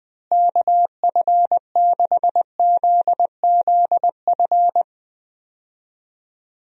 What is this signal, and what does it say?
Morse code, heard as a keyed tone, about 20 wpm, 700 Hz. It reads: KF6ZZF
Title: USA Call Signs - 6 Characters - Repeated 3x - 20wpm